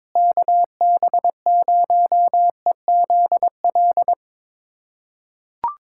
Morse code, heard as a keyed tone, about 22 wpm, 700 Hz. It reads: KB0EZL
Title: USA Callsign DE Callsign - 6 Characters - 22wpm